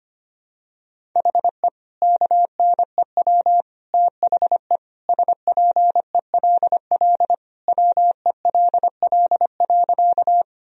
Morse code, heard as a keyed tone, about 25 wpm, 700 Hz. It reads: HE KNEW THE SPELL WELL.
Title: Sentences from Top 1000 Words - Review Part 1 - 25wpm